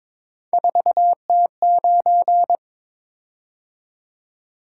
Morse code, heard as a keyed tone, about 23 wpm, 700 Hz. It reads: 4T9
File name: Three Letter-Number Combinations - 22wpm